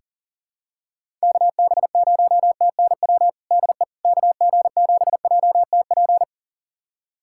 Morse code, heard as a keyed tone, about 40 wpm, 700 Hz. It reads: KB0TNW DE KG7JTP
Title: USA Callsign DE Callsign - 6 Characters - 40wpm